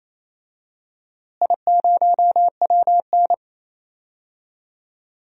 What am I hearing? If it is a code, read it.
Morse code, heard as a keyed tone, about 28 wpm, 700 Hz. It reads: I0WN
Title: DXCC-100 Call Signs - 4 Characters - Repeated 3x - 28wpm